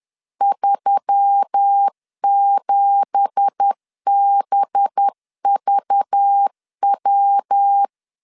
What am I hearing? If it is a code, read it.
Morse code, heard as a keyed tone, about 10 wpm, 800 Hz. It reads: 37BVW
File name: LEZIONI / 9 2867 10wpm 2733